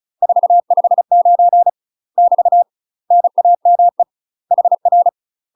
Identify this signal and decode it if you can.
Morse code, heard as a keyed tone, about 34 wpm, 700 Hz. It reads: 459 <BT> NAME HR
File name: UK MASTER QSOs TRAINING / 35WPM / UK MASTER QSOs 0037